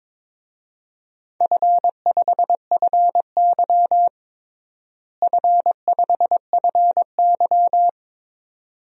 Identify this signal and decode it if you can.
Morse code, heard as a keyed tone, about 22 wpm, 700 Hz. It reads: F5FY F5FY
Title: DXCC-100 Call Signs - 4 Characters - Repeated 2x - 22wpm